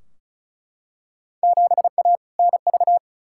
Morse code, heard as a keyed tone, about 36 wpm, 700 Hz. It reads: 7A NV
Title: Field Day Exchanges Encore - 35wpm